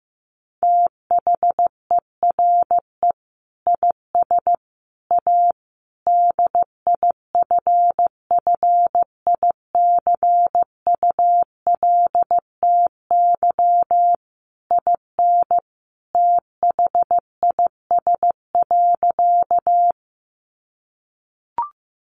Morse code, heard as a keyed tone, about 15 wpm, 700 Hz. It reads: THERE IS A DIFFICULTY IN THIS.
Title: Sentences from Top 1500 Words 15wpm